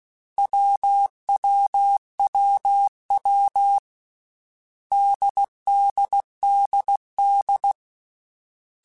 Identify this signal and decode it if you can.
Morse code, heard as a keyed tone, about 16 wpm, 800 Hz. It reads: WWWW DDDD
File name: Basic Sets / Basic Letter Groups Collective7 / AWUNDG@13